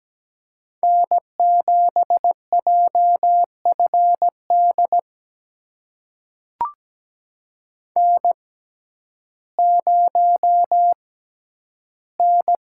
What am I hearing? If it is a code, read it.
Morse code, heard as a keyed tone, about 17 wpm, spaced as for 10 wpm, 700 Hz. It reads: N7JFD N0N
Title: USA Call Signs - 5 Characters - 6x Character Spacing - 17wpm